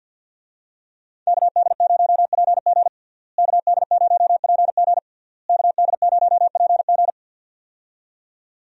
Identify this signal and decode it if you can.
Morse code, heard as a keyed tone, about 51 wpm, 700 Hz. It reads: KD0PG KD0PG KD0PG
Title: USA Call Signs - 5 Characters - Repeated 3x - 50wpm